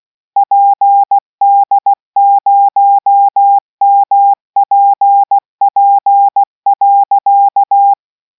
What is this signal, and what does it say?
Morse code, heard as a keyed tone, about 16 wpm, 800 Hz. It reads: PD0MPP.